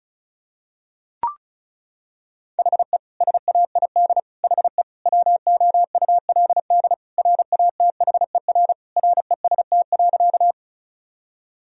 Morse code, heard as a keyed tone, about 36 wpm, 700 Hz. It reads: HE SAID HE WOULD RATHER REST.
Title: Sentences from Top 1000 Words - Review Part 4 - 35wpm